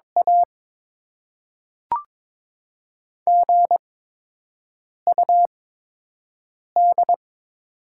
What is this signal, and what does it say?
Morse code, heard as a keyed tone, about 22 wpm, 700 Hz. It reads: A G U D
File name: Top 20 QSO Elements - 8x Character Spacing - 22wpm